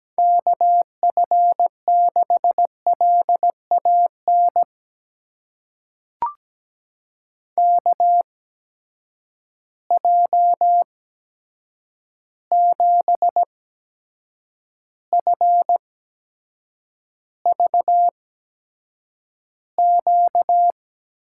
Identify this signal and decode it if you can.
Morse code, heard as a keyed tone, about 17 wpm, spaced as for 7 wpm, 700 Hz. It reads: KF6LAN KJ7FVQ
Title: USA Call Signs - 6 Characters - 8x Character Spacing - 17wpm